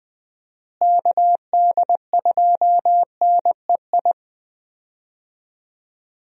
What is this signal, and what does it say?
Morse code, heard as a keyed tone, about 20 wpm, 700 Hz. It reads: KD2NEI
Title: USA Call Signs - 6 Characters - Repeated 2x - 20wpm